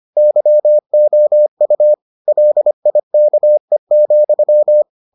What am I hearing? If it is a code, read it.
Morse code, heard as a keyed tone, about 25 wpm, 600 Hz. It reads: YOU LIKE,